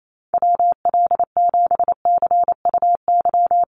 Morse code, heard as a keyed tone, about 28 wpm, 700 Hz. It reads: WL7CUY
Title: USA Call Signs Repeated 2x 28wpm